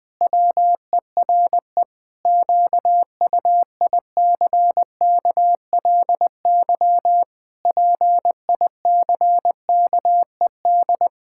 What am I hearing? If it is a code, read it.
Morse code, heard as a keyed tone, about 20 wpm, 700 Hz. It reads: WERE QUICKLY PICKED
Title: Pangram Challenge v2 - 20wpm